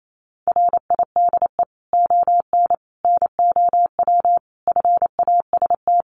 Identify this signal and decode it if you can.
Morse code, heard as a keyed tone, about 28 wpm, 700 Hz. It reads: RIDE ON NOW FAST
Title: Sentences from Top 1500 Words 28wpm